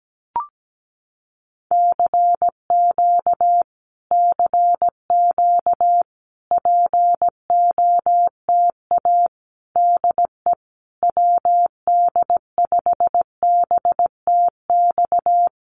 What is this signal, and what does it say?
Morse code, heard as a keyed tone, about 17 wpm, 700 Hz. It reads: CQ CQ POTA DE WD5BTX
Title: Common QSO Phrases Encore 17wpm